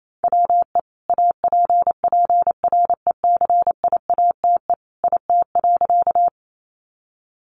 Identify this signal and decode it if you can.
Morse code, heard as a keyed tone, about 28 wpm, 700 Hz. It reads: WE APPRECIATE IT.
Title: Sentences from Top 1300 Words 28wpm